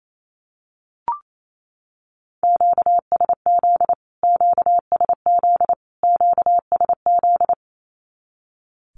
Morse code, heard as a keyed tone, about 28 wpm, 700 Hz. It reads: QSZ QSZ QSZ
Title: Q-Codes Repeated 3x 28wpm